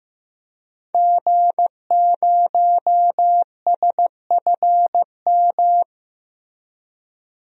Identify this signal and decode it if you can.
Morse code, heard as a keyed tone, about 15 wpm, 700 Hz. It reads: G0SFM
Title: DXCC-100 Call Signs - 5 Characters - Repeated 2x - 15wpm